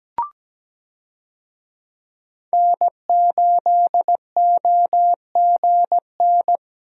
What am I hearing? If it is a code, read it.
Morse code, heard as a keyed tone, about 17 wpm, 700 Hz. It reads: N8OGN
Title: USA Call Signs - 5 Characters - Repeated 2x - 17wpm